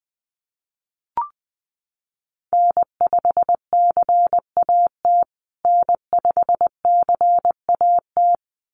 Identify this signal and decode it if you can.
Morse code, heard as a keyed tone, about 20 wpm, 700 Hz. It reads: N5CAT N5CAT
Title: USA Call Signs Repeated 2x 20wpm